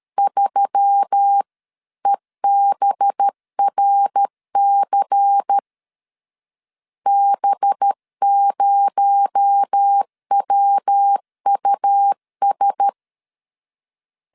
Morse code, heard as a keyed tone, about 13 wpm, 800 Hz. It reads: 3EBRC B0WUS
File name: CW / DET / 1 6 1500C S B 12wpm 3431